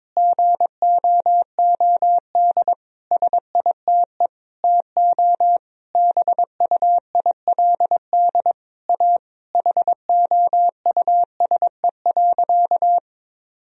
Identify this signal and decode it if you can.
Morse code, heard as a keyed tone, about 22 wpm, 700 Hz. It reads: GOOD SITE TO BUILD A HOUSE.
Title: Sentences from Top 1000 Words - Review Part 4 - 22wpm